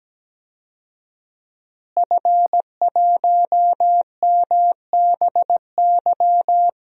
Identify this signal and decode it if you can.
Morse code, heard as a keyed tone, about 17 wpm, 700 Hz. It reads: F1MBY
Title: DXCC-100 Call Signs - 5 Characters - Repeated 3x - 17wpm